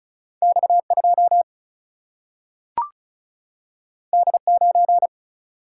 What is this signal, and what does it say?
Morse code, heard as a keyed tone, about 36 wpm, 700 Hz. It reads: X2 D9
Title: Single-Letter Single-Number 35wpm